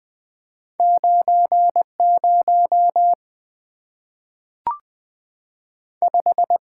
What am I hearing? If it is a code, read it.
Morse code, heard as a keyed tone, about 20 wpm, 700 Hz. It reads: 90 5
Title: Two Letter-Number Combinations - 20wpm